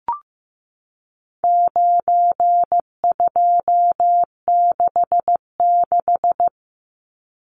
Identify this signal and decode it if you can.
Morse code, heard as a keyed tone, about 15 wpm, 700 Hz. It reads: 9266
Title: Numbers - 4-Digit 15wpm